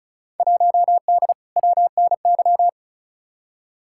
Morse code, heard as a keyed tone, about 36 wpm, 700 Hz. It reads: 1D WNY
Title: Field Day Exchanges - 35wpm